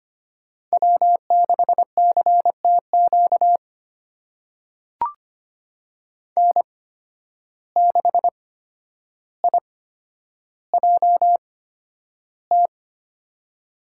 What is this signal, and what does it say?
Morse code, heard as a keyed tone, about 25 wpm, spaced as for 11 wpm, 700 Hz. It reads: W6CTQ N6IJT
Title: USA Call Signs - 5 Characters - 8x Character Spacing - 25wpm